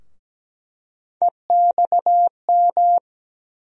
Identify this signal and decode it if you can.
Morse code, heard as a keyed tone, about 17 wpm, 700 Hz. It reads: EXM
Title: USA Call Sign Postfixes - 3 Letters 17wpm